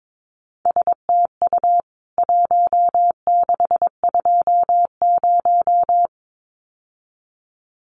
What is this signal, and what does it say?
Morse code, heard as a keyed tone, about 22 wpm, 700 Hz. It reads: STU 1620
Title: CWT Exchange 22wpm